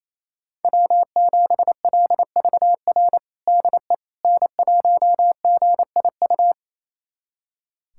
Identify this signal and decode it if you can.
Morse code, heard as a keyed tone, about 28 wpm, 700 Hz. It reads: W7LVR DE N1GIU
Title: USA Callsign DE Callsign - 5 Characters - 28wpm